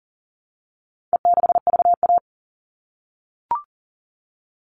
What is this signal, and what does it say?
Morse code, heard as a keyed tone, about 40 wpm, 700 Hz. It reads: E6VA
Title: All Cut Numbers - 4-Digit 40wpm